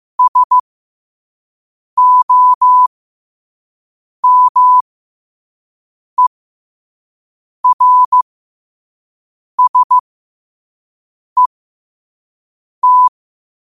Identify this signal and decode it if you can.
Morse code, heard as a keyed tone, about 15 wpm, spaced as for 5 wpm, 1000 Hz. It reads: SOMERSET
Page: 5 WPM morse code quotes